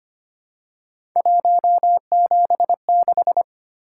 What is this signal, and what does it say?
Morse code, heard as a keyed tone, about 25 wpm, 700 Hz. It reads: 176
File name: Three Letter-Number Combinations - 25wpm